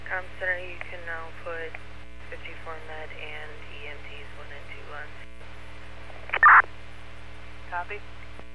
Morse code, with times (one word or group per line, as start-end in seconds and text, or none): none